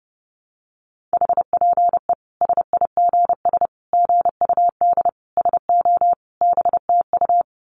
Morse code, heard as a keyed tone, about 30 wpm, 700 Hz. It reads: HPE SIGS GUD SO BTU
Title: Common QSO Phrases 30wpm